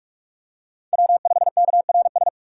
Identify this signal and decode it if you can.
Morse code, heard as a keyed tone, about 46 wpm, 700 Hz. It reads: W5KRS
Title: USA Call Signs - 4x Character Spacing - 45wpm